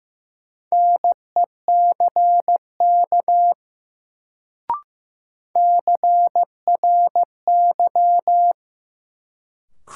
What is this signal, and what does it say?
Morse code, heard as a keyed tone, about 15 wpm, 700 Hz. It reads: NECK CRY
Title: Sets of 5 Words Encore - New Word Drill - 15wpm